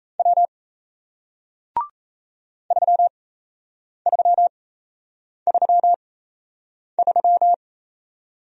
Morse code, heard as a keyed tone, about 38 wpm, spaced as for 8 wpm, 700 Hz. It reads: W 3333